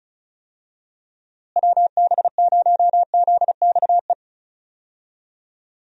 Morse code, heard as a keyed tone, about 36 wpm, 700 Hz. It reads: WB0ZXE
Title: USA Call Signs - 6 Characters - Repeated 3x - 35wpm